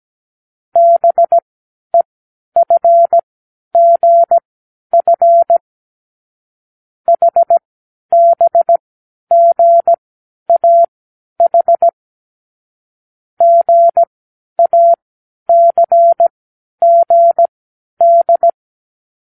Morse code, heard as a keyed tone, about 17 wpm, spaced as for 10 wpm, 680 Hz. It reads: BEFGF HBGAH GACGD
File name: k7qo / Lesson12(A-H)